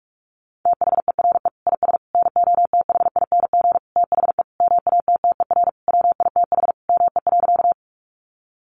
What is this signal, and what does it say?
Morse code, heard as a keyed tone, about 44 wpm, 700 Hz. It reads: THERE IS NOTHING THE MATTER WITH ME.
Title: Sentences from Top 700 Words 45wpm